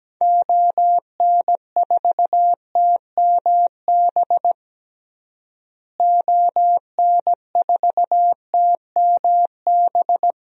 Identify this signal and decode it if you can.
Morse code, heard as a keyed tone, about 17 wpm, 700 Hz. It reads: ON4TMB ON4TMB
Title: DXCC-100 Call Signs - 6 Characters - Repeated 2x - 17wpm